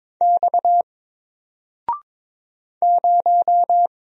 Morse code, heard as a keyed tone, about 22 wpm, 700 Hz. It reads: X 0
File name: Single-Letter-Number-22wpm